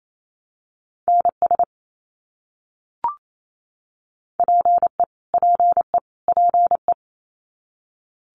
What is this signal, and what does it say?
Morse code, heard as a keyed tone, about 28 wpm, 700 Hz. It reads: NS PE PE PE
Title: Canadian Provinces and Territories - Repeated 3x - 28wpm